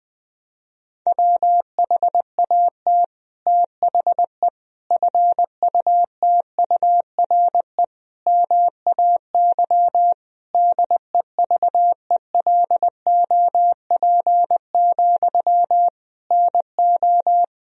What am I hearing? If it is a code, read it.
Morse code, heard as a keyed tone, about 19 wpm, 700 Hz. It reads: WHAT THE FUTURE MAY DEVELOP, NO
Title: Sentences from Top 1000 Words - Review Part 2 - 20wpm